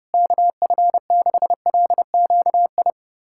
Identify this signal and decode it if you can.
Morse code, heard as a keyed tone, about 30 wpm, 700 Hz. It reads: KF6LQI
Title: USA Callsign DE Callsign - 6 Characters - 30wpm